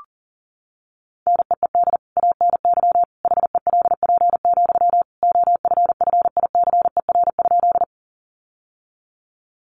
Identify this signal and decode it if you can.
Morse code, heard as a keyed tone, about 40 wpm, 700 Hz. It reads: NEED ANY HELP, OFFICER?
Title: Sentences from Top 1200 Words 40wpm